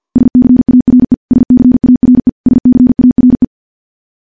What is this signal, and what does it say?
Morse code, heard as a keyed tone, about 46 wpm, 250 Hz. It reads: SCARE SCARE SCARE
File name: morse / morse-training-50WPM